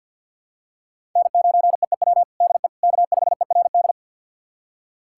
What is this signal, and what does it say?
Morse code, heard as a keyed tone, about 52 wpm, 700 Hz. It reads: N9EEW DE K5ERD
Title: USA Callsign DE Callsign - 5 Characters - 50wpm